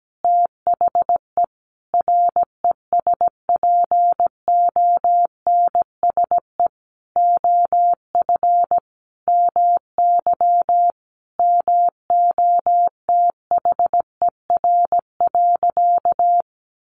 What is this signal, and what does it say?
Morse code, heard as a keyed tone, about 17 wpm, 700 Hz. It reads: THE RESPONSE OF MY MOTHER.
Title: Sentences from Top 1400 Words 17wpm